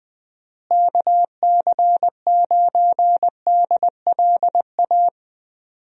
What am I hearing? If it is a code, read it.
Morse code, heard as a keyed tone, about 21 wpm, 700 Hz. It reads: KC9DLA
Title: USA Call Signs - 6 Characters - Repeated 3x - 20wpm